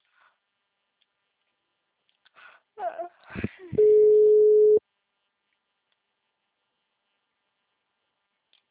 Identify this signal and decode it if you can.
Ringback tone (Europe, 425 Hz), with speech.